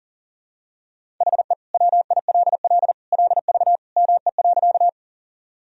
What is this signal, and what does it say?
Morse code, heard as a keyed tone, about 40 wpm, 700 Hz. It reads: HE WILL LV ME.
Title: Sentences from Top 400 Words and Abbreviations 40wpm